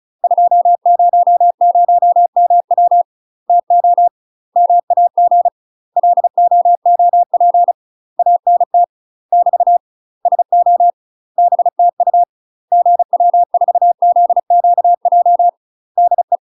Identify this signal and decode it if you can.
Morse code, heard as a keyed tone, about 36 wpm, 700 Hz. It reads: 200MW TO MAG LOOP ANT <BT> SO BTU GW4ZQJ DE
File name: UK MASTER QSOs TRAINING / 35WPM / UK MASTER QSOs 0000